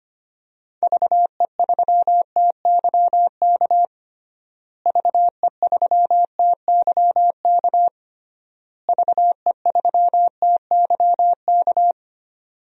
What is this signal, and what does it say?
Morse code, heard as a keyed tone, about 25 wpm, 700 Hz. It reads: VE3TYK VE3TYK VE3TYK
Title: DXCC-100 Call Signs - 6 Characters - Repeated 3x - 25wpm